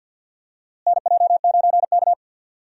Morse code, heard as a keyed tone, about 47 wpm, 700 Hz. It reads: NJ9K
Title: USA Call Signs - 4 Characters - Repeated 2x - 50wpm